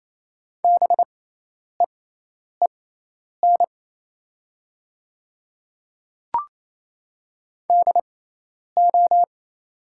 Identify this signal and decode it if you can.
Morse code, heard as a keyed tone, about 28 wpm, spaced as for 10 wpm, 700 Hz. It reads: BEEN DO
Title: 6x Character Spacing - Top 100 Words - 28wpm